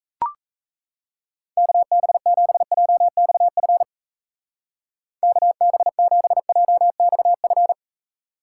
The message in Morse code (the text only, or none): KB7JXF KB7JXF